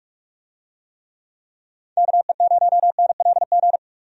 Morse code, heard as a keyed tone, about 46 wpm, 700 Hz. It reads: KE0NLG
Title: USA Call Signs - 6 Characters - 45wpm